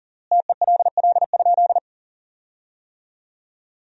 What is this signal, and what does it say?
Morse code, heard as a keyed tone, about 40 wpm, 700 Hz. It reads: TELL?
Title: 3x Word-Spacing - Sentences from Top 200 Words - 40wpm